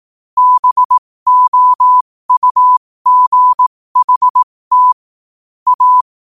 Morse code, heard as a keyed tone, about 18 wpm, spaced as for 15 wpm, 1000 Hz. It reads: BOUGHT A
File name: qotdfiles / 15wpm